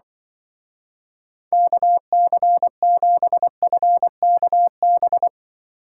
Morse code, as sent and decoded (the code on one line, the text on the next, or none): -.- -.-. --... ..-. -.- -...
KC7FKB